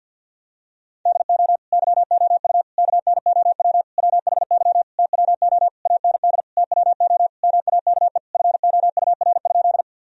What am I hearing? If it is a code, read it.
Morse code, heard as a keyed tone, about 51 wpm, 700 Hz. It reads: DO YOU KNOW WHY TWO AND TWO MAKE FOUR?
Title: Sentences from Top 1000 Words - Review Part 1 - 50wpm